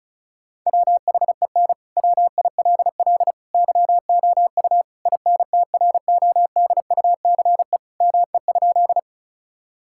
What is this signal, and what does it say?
Morse code, heard as a keyed tone, about 34 wpm, 700 Hz. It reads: WHEN WILL YOU INTRODUCE ME?
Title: Sentences from Top 1600 Words 35wpm